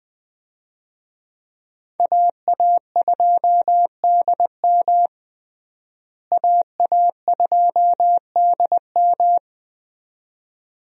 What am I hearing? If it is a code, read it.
Morse code, heard as a keyed tone, about 20 wpm, 700 Hz. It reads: AA2DM AA2DM
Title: USA Call Signs - 5 Characters - Repeated 2x - 20wpm